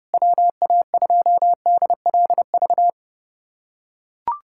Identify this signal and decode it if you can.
Morse code, heard as a keyed tone, about 31 wpm, 700 Hz. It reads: WA2DLV
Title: USA Callsign DE Callsign - 6 Characters - 30wpm